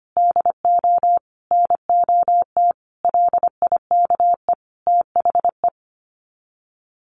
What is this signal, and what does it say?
Morse code, heard as a keyed tone, about 25 wpm, 700 Hz. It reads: DO NOT LIKE THE
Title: Sets of 5 Words Encore 25wpm